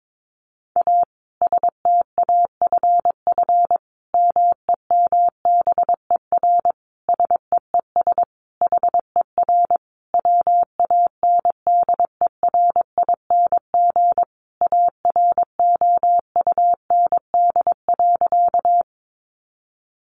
Morse code, heard as a keyed tone, about 22 wpm, 700 Hz. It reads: A STAFF MEMBER SEES HER WANDERING AROUND.
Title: Additional Easy Phrases 22wpm